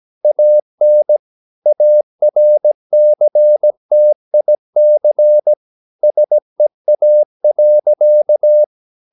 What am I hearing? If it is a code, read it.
Morse code, heard as a keyed tone, about 17 wpm, 600 Hz. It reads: AN ARCTIC SEA.
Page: Romanen Never af Ken Follet på engelsk.